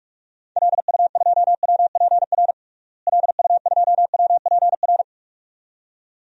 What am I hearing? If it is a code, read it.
Morse code, heard as a keyed tone, about 44 wpm, 690 Hz. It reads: LU2WPR LU2WPR
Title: DXCC-100 Call Signs - 6 Characters - Repeated 2x - 45wpm